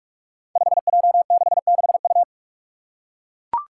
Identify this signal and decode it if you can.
Morse code, heard as a keyed tone, about 44 wpm, 700 Hz. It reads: 5J66U
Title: Five Challenging Letter-Number Combinations 45wpm